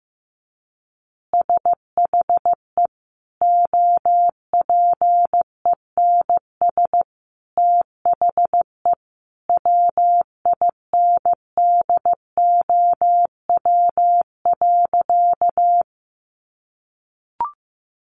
Morse code, heard as a keyed tone, about 14 wpm, 700 Hz. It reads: SHE OPENS THE WINDOW.
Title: Additional Easy Phrases 15wpm